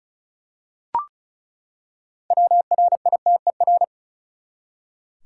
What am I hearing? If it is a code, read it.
Morse code, heard as a keyed tone, about 36 wpm, 700 Hz. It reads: WRITER
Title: Six Letter Words - 35wpm